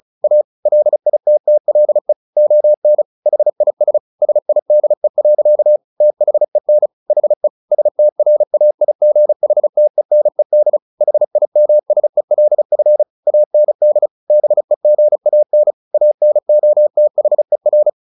Morse code, heard as a keyed tone, about 34 wpm, 600 Hz. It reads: A LITTLE ON HIS SIDE. THEN HE STRAIGHTENED HIMSELF AND BEGAN ANOTHER
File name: the old man and the sea 2 35wpm